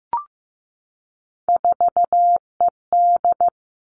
Morse code, heard as a keyed tone, about 15 wpm, 700 Hz. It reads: 4ED
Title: Three Challenging Letter-Number Combinations 15wpm